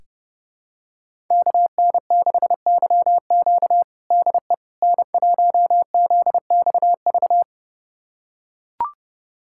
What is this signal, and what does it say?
Morse code, heard as a keyed tone, about 32 wpm, 700 Hz. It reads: KN6YQ DE N1ZXV
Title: USA Callsign DE Callsign - 5 Characters - 30wpm